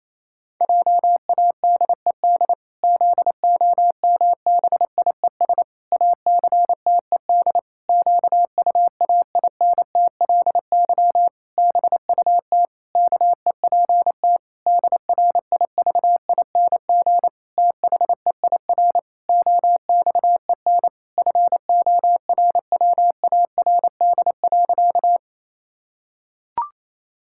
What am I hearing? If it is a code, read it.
Morse code, heard as a keyed tone, about 28 wpm, 700 Hz. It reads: JADED ZOMBIES ACTED QUAINTLY BUT KEPT DRIVING THEIR OXEN FORWARD.
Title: Pangram Challenge v2 - 28wpm